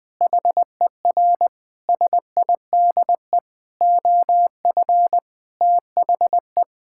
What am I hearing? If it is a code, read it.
Morse code, heard as a keyed tone, about 20 wpm, 700 Hz. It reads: HER SIDE OF THE
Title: Sentences from Top 1000 Words - Review Part 6 - 20wpm